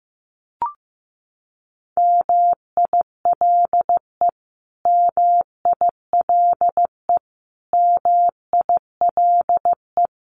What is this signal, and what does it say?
Morse code, heard as a keyed tone, about 15 wpm, 700 Hz. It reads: MILE MILE MILE
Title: Top 500 Words Repeated 15wpm